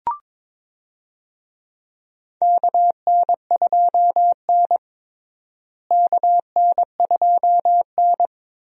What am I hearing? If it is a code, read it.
Morse code, heard as a keyed tone, about 22 wpm, 700 Hz. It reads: KN2N KN2N
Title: USA Call Signs - 4 Characters - Repeated 3x - 22wpm